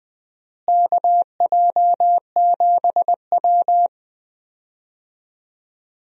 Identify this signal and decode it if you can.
Morse code, heard as a keyed tone, about 20 wpm, 700 Hz. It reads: KJ7W
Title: USA Call Signs - 4 Characters - Repeated 2x - 20wpm